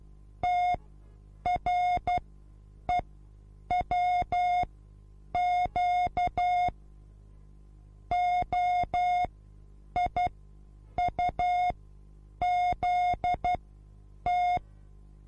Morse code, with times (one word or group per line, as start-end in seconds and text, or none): 0.4-6.7 TREWQ
8.1-14.6 OIUZT